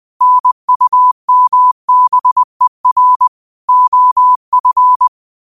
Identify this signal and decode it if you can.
Morse code, heard as a keyed tone, about 20 wpm, 1000 Hz. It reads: NUMBER OF
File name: qotdfiles / 20wpm